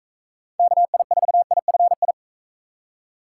Morse code, heard as a keyed tone, about 38 wpm, 700 Hz. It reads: KI4IFI